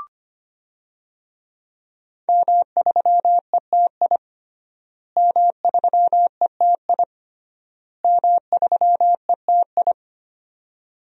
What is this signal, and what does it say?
Morse code, heard as a keyed tone, about 25 wpm, 700 Hz. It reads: M3ETI M3ETI M3ETI
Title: DXCC-100 Call Signs - 5 Characters - Repeated 3x - 25wpm